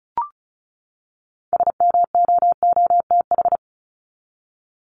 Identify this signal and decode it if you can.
Morse code, heard as a keyed tone, about 34 wpm, 700 Hz. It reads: SMOOTH
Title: Top 1251-1500 Words 35wpm